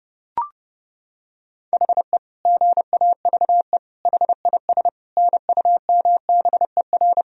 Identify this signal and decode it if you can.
Morse code, heard as a keyed tone, about 30 wpm, 700 Hz. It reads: HE GAVE HIS NUMBER
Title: Sentences from Top 1000 Words - Review Part 1 - 30wpm